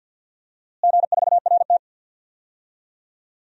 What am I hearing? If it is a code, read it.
Morse code, heard as a keyed tone, about 52 wpm, 700 Hz. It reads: G4RT
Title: DXCC-100 Call Signs - 4 Characters - Repeated 2x - 50wpm